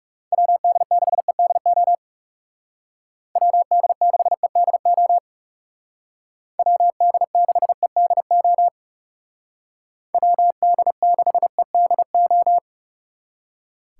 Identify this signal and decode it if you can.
Morse code, heard as a keyed tone, about 36 wpm, 700 Hz. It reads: WD6EDO WD6EDO WD6EDO WD6EDO
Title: Speed Racing - USA 2x3 Call Signs 30wpm